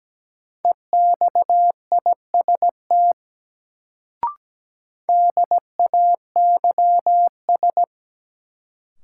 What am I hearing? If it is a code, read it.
Morse code, heard as a keyed tone, about 17 wpm, 700 Hz. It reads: EXIST DAYS
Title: Sentences from Top 1000 Words - Review Part 5 - New Word Drill - 17wpm